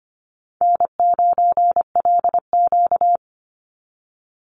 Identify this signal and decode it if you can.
Morse code, heard as a keyed tone, about 25 wpm, 700 Hz. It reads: N9LQ
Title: Easy Call Signs - 25wpm